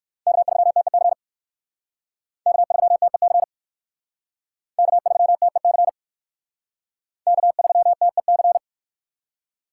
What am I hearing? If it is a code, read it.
Morse code, heard as a keyed tone, about 60 wpm, 700 Hz. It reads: K3TEC K3TEC K3TEC K3TEC